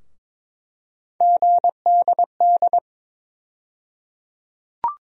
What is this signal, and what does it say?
Morse code, heard as a keyed tone, about 22 wpm, 700 Hz. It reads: GDD
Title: USA Call Sign Postfixes - 3 Letters 22wpm